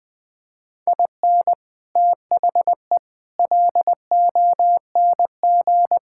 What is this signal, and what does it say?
Morse code, heard as a keyed tone, about 20 wpm, 700 Hz. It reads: IN THE LONG
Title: Sentences with 4-Word Phrases - 20wpm